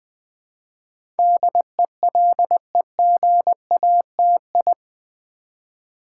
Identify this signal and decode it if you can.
Morse code, heard as a keyed tone, about 20 wpm, 700 Hz. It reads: DELEGATI
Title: Build A Word - 4 Letters At A Time - 10-Letter Words - 20wpm